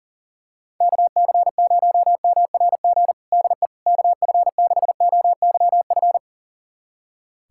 Morse code, heard as a keyed tone, about 37 wpm, 700 Hz. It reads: KC0MRG DE KF6OYF
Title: USA Callsign DE Callsign - 6 Characters - 40wpm